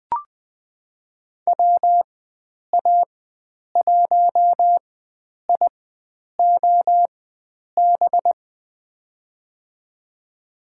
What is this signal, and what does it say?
Morse code, heard as a keyed tone, about 20 wpm, spaced as for 9 wpm, 700 Hz. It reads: WA1IOB
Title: USA Call Signs - 4x Character Spacing - 20wpm